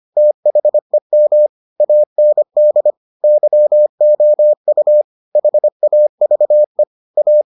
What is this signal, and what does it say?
Morse code, heard as a keyed tone, about 25 wpm, 600 Hz. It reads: THEM AND YOU HAVE A